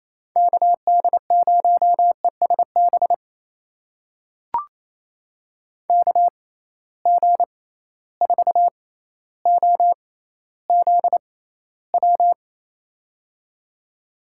Morse code, heard as a keyed tone, about 28 wpm, spaced as for 14 wpm, 700 Hz. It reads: KD0ESB KG4OZW
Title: USA Call Signs - 6 Characters - 6x Character Spacing - 28wpm